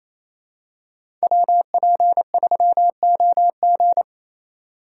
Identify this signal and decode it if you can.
Morse code, heard as a keyed tone, about 28 wpm, 700 Hz. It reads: WP3OG
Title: USA Call Signs - 5 Characters - 4x Character Spacing - 28wpm